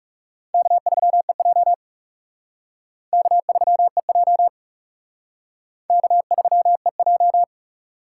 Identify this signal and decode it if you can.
Morse code, heard as a keyed tone, about 34 wpm, 700 Hz. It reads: K3EJ K3EJ K3EJ
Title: Speed Racing - USA 1x2 Call Signs 30wpm